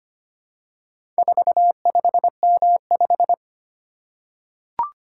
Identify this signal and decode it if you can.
Morse code, heard as a keyed tone, about 25 wpm, 700 Hz. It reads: 45M5
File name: practice / 25wpm / Four Letter-Number Combinations - 25wpm